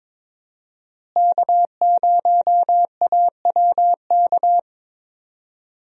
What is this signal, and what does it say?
Morse code, heard as a keyed tone, about 22 wpm, 700 Hz. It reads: K0AWK
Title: USA Call Signs - 5 Characters - Repeated 3x - 22wpm